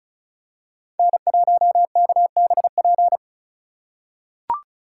Morse code, heard as a keyed tone, about 36 wpm, 700 Hz. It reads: N1KBP
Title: USA Call Signs - 5 Characters - 3x Character Spacing - 35wpm